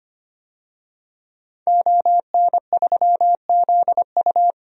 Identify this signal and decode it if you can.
Morse code, heard as a keyed tone, about 25 wpm, 700 Hz. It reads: ON3ZU
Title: DXCC-100 Call Signs - 5 Characters - 25wpm